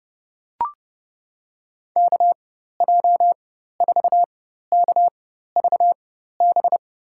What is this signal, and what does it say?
Morse code, heard as a keyed tone, about 31 wpm, spaced as for 14 wpm, 700 Hz. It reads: KJ4KVB
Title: USA Call Signs - 6 Characters - 4x Character Spacing - 30wpm